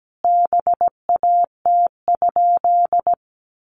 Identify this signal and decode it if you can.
Morse code, heard as a keyed tone, about 17 wpm, 700 Hz. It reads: BAT?
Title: Sentences from Top 1500 Words 17wpm